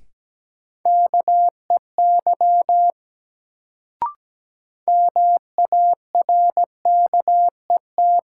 Morse code, heard as a keyed tone, about 17 wpm, 700 Hz. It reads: KEY MARKET
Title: Sentences from Top 600 Words - New Word Drill - 17wpm